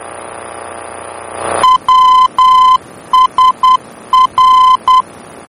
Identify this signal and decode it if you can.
Morse code, heard as a keyed tone, about 10 wpm, 1020 Hz. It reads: WSR
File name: germany / mx / WSR-D-VOR